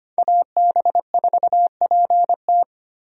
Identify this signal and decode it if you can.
Morse code, heard as a keyed tone, about 25 wpm, 700 Hz. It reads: AB4PT
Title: USA Call Signs - 5 Characters - Repeated 3x - 25wpm